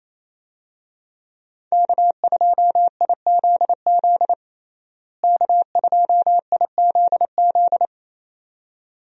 Morse code, heard as a keyed tone, about 28 wpm, 700 Hz. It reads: K2IZZ K2IZZ
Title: USA Call Signs - 5 Characters - Repeated 2x - 28wpm